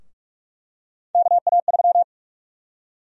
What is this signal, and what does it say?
Morse code, heard as a keyed tone, about 42 wpm, 700 Hz. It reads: KA3
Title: USA Call Sign Prefixes - Repeated 2x - 45wpm